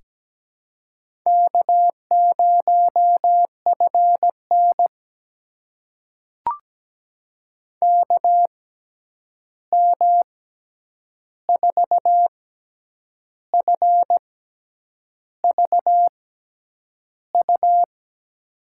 Morse code, heard as a keyed tone, about 17 wpm, spaced as for 8 wpm, 700 Hz. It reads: K0FN KM4FVU
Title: USA Call Signs - 6x Character Spacing - 17wpm